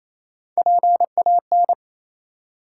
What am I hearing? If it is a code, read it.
Morse code, heard as a keyed tone, about 27 wpm, 700 Hz. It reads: PAN
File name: Easy Words - Speed-Racing with Hints - 22wpm